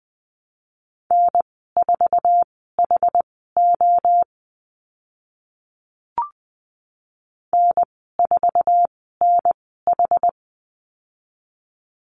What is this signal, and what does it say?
Morse code, heard as a keyed tone, about 20 wpm, spaced as for 14 wpm, 700 Hz. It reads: N4HO N4NH
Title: 069 Lesson 12g - Letter d - Call Signs 20wpm